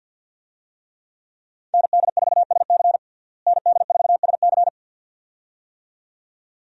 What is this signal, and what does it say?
Morse code, heard as a keyed tone, about 52 wpm, 700 Hz. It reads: ND4SC ND4SC
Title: USA Call Signs - 5 Characters - Repeated 2x - 50wpm